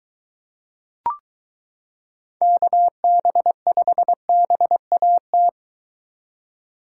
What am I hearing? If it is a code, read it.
Morse code, heard as a keyed tone, about 23 wpm, 700 Hz. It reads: KB5BAT